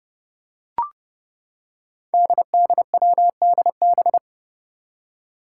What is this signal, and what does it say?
Morse code, heard as a keyed tone, about 31 wpm, 700 Hz. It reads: DDWDB
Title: Five Challenging Letter-Number Combinations 30wpm